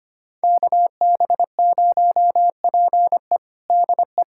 Morse code, heard as a keyed tone, about 25 wpm, 700 Hz. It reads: KB0PE DE
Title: USA Callsign DE Callsign - 5 Characters - 25wpm